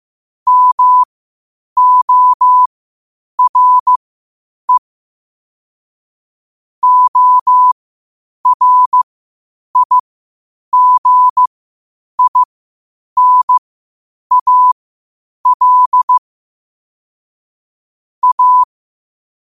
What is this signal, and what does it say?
Morse code, heard as a keyed tone, about 15 wpm, spaced as for 8 wpm, 1000 Hz. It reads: MORE ORIGINAL A
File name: qotdfiles / 8wpm